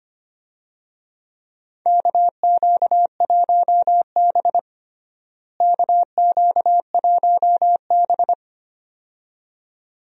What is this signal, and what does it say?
Morse code, heard as a keyed tone, about 25 wpm, 700 Hz. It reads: KQ1B KQ1B
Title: USA Call Signs - 4 Characters - Repeated 2x - 25wpm